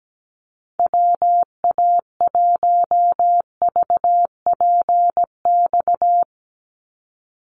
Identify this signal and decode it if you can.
Morse code, heard as a keyed tone, about 17 wpm, 700 Hz. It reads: WA1VPX
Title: USA Call Signs 17wpm